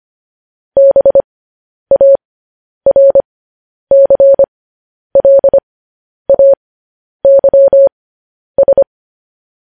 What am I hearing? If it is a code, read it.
Morse code, heard as a keyed tone, about 25 wpm, spaced as for 10 wpm, 550 Hz. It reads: BARCLAYS
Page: news headlines in morse code at 10 wpm